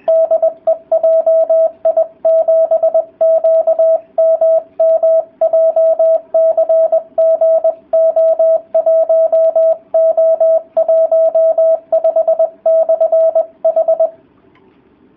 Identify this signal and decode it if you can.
Morse code, heard as a keyed tone, about 20 wpm, 650 Hz, with radio noise. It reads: DE JI7QMM JCG O1O15/H